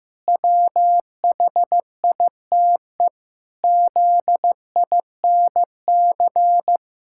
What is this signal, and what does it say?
Morse code, heard as a keyed tone, about 15 wpm, 700 Hz. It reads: WHITE ZINC
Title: Pangram Challenge v2 - 15wpm